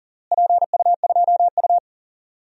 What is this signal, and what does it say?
Morse code, heard as a keyed tone, about 40 wpm, 700 Hz. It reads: PU2U
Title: DXCC-100 Call Signs - 4 Characters - Repeated 3x - 40wpm